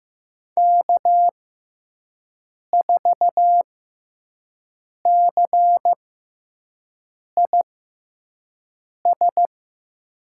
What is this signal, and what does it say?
Morse code, heard as a keyed tone, about 15 wpm, spaced as for 5 wpm, 700 Hz. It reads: K4CIS
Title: USA Call Signs - 6x Character Spacing - 15wpm